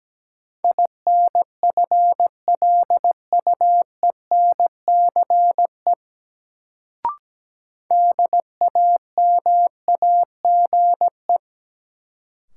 Morse code, heard as a keyed tone, about 17 wpm, 700 Hz. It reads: INFLUENCE DAMAGE
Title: Sentences from Top 1100 Words - New Word Drill Alt - 17wpm